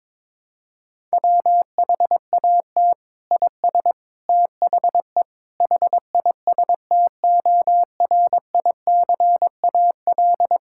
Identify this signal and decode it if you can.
Morse code, heard as a keyed tone, about 22 wpm, 700 Hz. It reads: WHAT IS THE HISTORICAL
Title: Sentences from Top 1000 Words - Review Part 5 - 22wpm